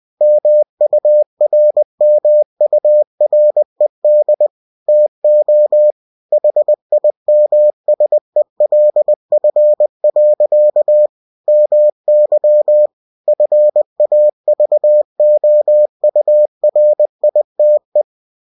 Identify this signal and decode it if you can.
Morse code, heard as a keyed tone, about 20 wpm, 600 Hz. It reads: MURMURED TO HIMSELF. MY FAVOURITE
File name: cw-books / winnie-the-pooh / winnie-20wpm-page-0056